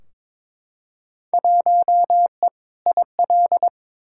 Morse code, heard as a keyed tone, about 22 wpm, 700 Hz. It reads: 1E IL
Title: Field Day Exchanges - 22wpm